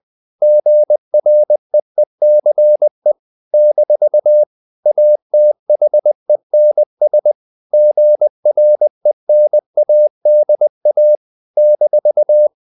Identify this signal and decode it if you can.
Morse code, heard as a keyed tone, about 20 wpm, 600 Hz. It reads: GREECE - ATHENS GRENADA -
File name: CW / Practice / NationalCapitals20